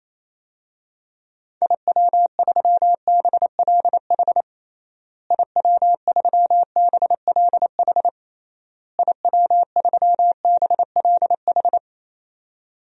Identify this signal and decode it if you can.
Morse code, heard as a keyed tone, about 28 wpm, 700 Hz. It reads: IW3BLH IW3BLH IW3BLH
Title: DXCC-100 Call Signs - 6 Characters - Repeated 3x - 28wpm